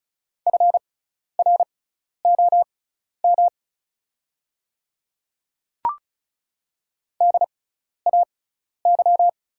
Morse code, heard as a keyed tone, about 36 wpm, spaced as for 12 wpm, 690 Hz. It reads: FROM DAY
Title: 6x Character Spacing - Top 100 Words - 35wpm